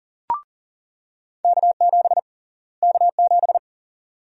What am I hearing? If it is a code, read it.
Morse code, heard as a keyed tone, about 40 wpm, 700 Hz. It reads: K7 K7
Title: USA Call Sign Prefixes - Repeated 2x - 40wpm